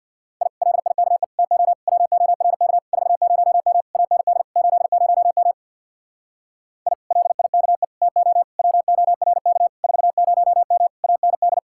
Morse code, heard as a keyed tone, about 64 wpm, 700 Hz. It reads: I LIKE TO WORK 40M AND 80M I LIKE TO WORK 40M AND